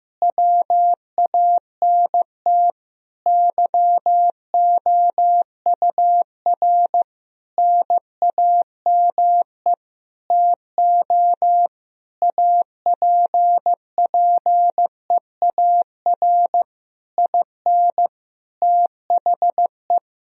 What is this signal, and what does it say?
Morse code, heard as a keyed tone, about 15 wpm, 700 Hz. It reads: WANT YOUR NAME TO APPEAR IN THE
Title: Sentences from Top 1000 Words - Review Part 2 - 15wpm